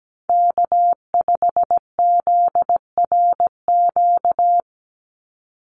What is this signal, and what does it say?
Morse code, heard as a keyed tone, about 17 wpm, 700 Hz. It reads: K5ZRQ
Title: USA Call Signs 17wpm